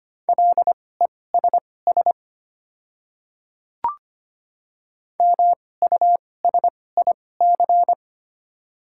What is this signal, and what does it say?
Morse code, heard as a keyed tone, about 25 wpm, spaced as for 18 wpm, 700 Hz. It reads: LESS MUSIC
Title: 2x Character Spacing - Top 500 Words - 25wpm